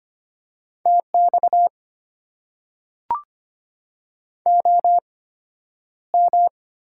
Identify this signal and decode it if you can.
Morse code, heard as a keyed tone, about 25 wpm, spaced as for 11 wpm, 700 Hz. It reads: TX OM
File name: Top 90 QSO Elements - 8x Character Spacing - 25wpm